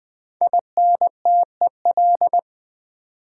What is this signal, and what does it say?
Morse code, heard as a keyed tone, about 20 wpm, 700 Hz. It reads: INTEL
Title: Build A Word - 5 Letters At A Time - 12-Letter Words - 20wpm